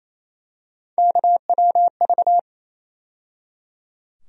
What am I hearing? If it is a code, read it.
Morse code, heard as a keyed tone, about 28 wpm, 700 Hz. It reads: KWV
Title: USA Call Sign Postfixes - 3 Letters 28wpm